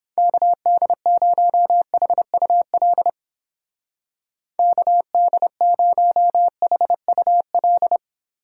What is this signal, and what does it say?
Morse code, heard as a keyed tone, about 30 wpm, 700 Hz. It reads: KD0HUL KD0HUL